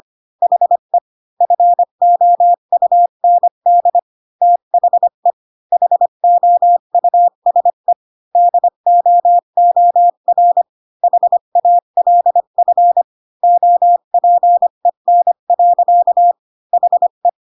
Morse code, heard as a keyed tone, about 25 wpm, 700 Hz. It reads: HE FOUND THE HOUSE DOOR HALF OPEN. HE
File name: Pinocchio / Pinocchio ch4 20 25 ik1hge